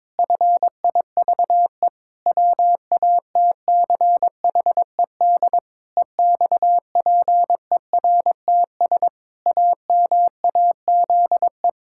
Morse code, heard as a keyed tone, about 21 wpm, 700 Hz. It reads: FIVE WATCHED EXPERTS AMAZE
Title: Pangram Challenge v2 - 22wpm